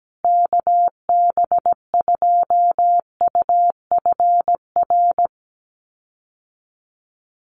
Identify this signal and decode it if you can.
Morse code, heard as a keyed tone, about 17 wpm, 700 Hz. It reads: KB2UFR
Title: USA Call Signs Repeated 3x 17wpm